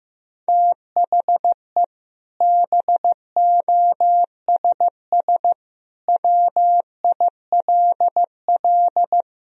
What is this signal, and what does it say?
Morse code, heard as a keyed tone, about 16 wpm, 700 Hz. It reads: THE BOSS WILL
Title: Sentences from Top 1000 Words - Review Part 4 - 15wpm